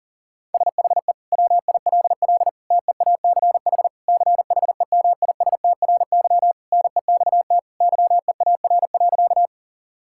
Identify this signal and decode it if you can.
Morse code, heard as a keyed tone, about 40 wpm, 700 Hz. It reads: SHE WILL TEACH CHEMISTRY NEXT YEAR.
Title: Sentences from Top 1000 Words - Review Part 6 - 40wpm